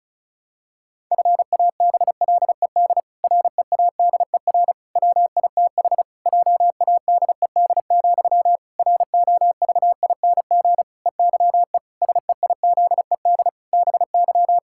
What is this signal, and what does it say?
Morse code, heard as a keyed tone, about 36 wpm, 700 Hz. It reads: FABLED READER WITH JADED, ROVING EYE SEIZED BY
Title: Pangram Challenge v2 - 35wpm